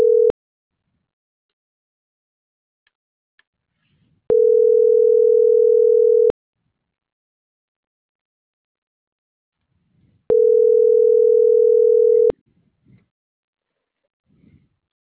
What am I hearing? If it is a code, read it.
Ringback tone (North America), 440 + 480 Hz.